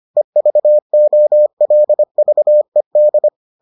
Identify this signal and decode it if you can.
Morse code, heard as a keyed tone, about 25 wpm, 600 Hz. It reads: EVOLVED